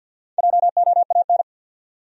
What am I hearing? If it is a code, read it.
Morse code, heard as a keyed tone, about 50 wpm, 700 Hz. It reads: JOAN
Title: USA Common Names - Max 4 Letters - 50wpm